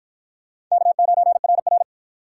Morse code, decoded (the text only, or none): K9RR